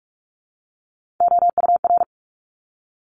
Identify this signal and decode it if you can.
Morse code, heard as a keyed tone, about 44 wpm, 700 Hz. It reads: OUR
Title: Top 300 Words 45wpm